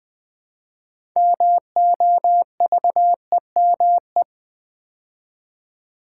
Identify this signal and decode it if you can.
Morse code, heard as a keyed tone, about 20 wpm, 700 Hz. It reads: MOVEME
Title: Build A Word - 2 Letters At A Time - 8-Letter Words - 20wpm